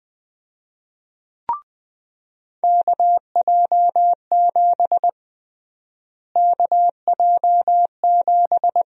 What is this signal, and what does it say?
Morse code, heard as a keyed tone, about 20 wpm, 700 Hz. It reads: KJ7 KJ7
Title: USA Call Sign Prefixes - Repeated 2x - 20wpm